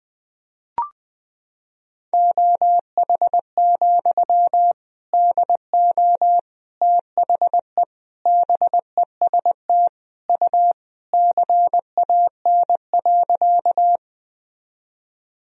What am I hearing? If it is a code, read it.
Morse code, heard as a keyed tone, about 21 wpm, 700 Hz. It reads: OH, DO THE BEST U CAN.
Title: Sentences from Top 500 Words and Abbreviations 20wpm